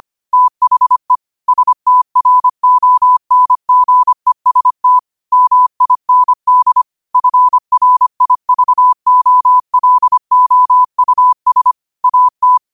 Morse code, heard as a keyed tone, about 25 wpm, 1000 Hz. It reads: THE STRONGEST MIND FRIVOLOUS AT
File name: qotdfiles / 25wpm